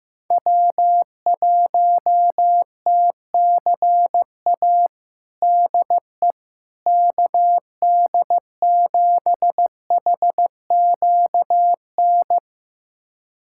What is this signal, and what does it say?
Morse code, heard as a keyed tone, about 15 wpm, 700 Hz. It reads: W1TCA DE KD7HQN
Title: USA Callsign DE Callsign 15wpm